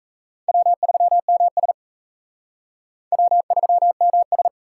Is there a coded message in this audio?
Yes: Morse code from 0.5 to 4.5 s, heard as a keyed tone, about 36 wpm, 700 Hz. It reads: W3MS W3MS